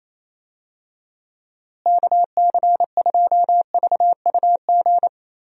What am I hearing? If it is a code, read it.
Morse code, heard as a keyed tone, about 28 wpm, 700 Hz. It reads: KC2VUG
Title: USA Call Signs - 6 Characters - Repeated 3x - 28wpm